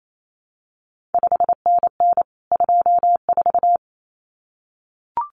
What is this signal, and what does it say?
Morse code, heard as a keyed tone, about 28 wpm, 700 Hz. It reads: 5NN 24
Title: CQ WW DX Contest Exchange Encore - 28wpm